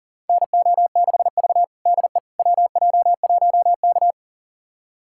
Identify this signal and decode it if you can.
Morse code, heard as a keyed tone, about 40 wpm, 700 Hz. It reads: NO6V DE WJ1K
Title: USA Callsign DE Callsign - 4 Characters - 40wpm